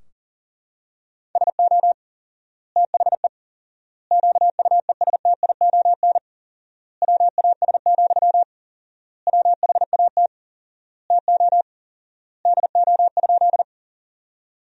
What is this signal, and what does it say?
Morse code, heard as a keyed tone, about 40 wpm, 700 Hz. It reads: SO THE QUESTION WAS, WHAT TO DO?
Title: 4x Word-Spacing - Sentences from Top 400 Words - 40wpm